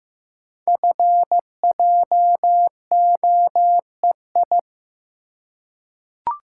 Morse code, heard as a keyed tone, about 15 wpm, 700 Hz. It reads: FJOEI
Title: Code Groups - 5 Letters - 15wpm